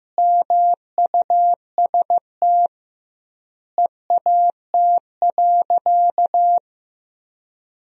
Morse code, heard as a keyed tone, about 15 wpm, 700 Hz. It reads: MUST EAT.
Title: 2x Word-Spacing - Sentences from Top 300 Words - 15wpm